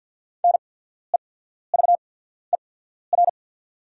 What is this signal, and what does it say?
Morse code, heard as a keyed tone, about 48 wpm, spaced as for 14 wpm, 690 Hz. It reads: NEVER
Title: 8x Character Spacing - Top 500 Words - 50wpm